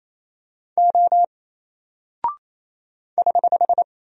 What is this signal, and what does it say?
Morse code, heard as a keyed tone, about 27 wpm, 700 Hz. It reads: O <HH>
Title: Single Character (Letter:Number:Punctuation:Prosign) 28wpm